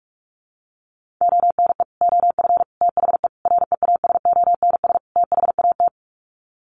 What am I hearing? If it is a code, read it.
Morse code, heard as a keyed tone, about 44 wpm, 700 Hz. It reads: ONE OF THE REASONS THAT
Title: Sets of 5 Words Encore 45wpm